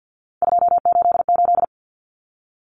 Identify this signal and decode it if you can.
Morse code, heard as a keyed tone, about 50 wpm, 700 Hz. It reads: 288
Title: Common Cut Numbers - 3-Digit 50wpm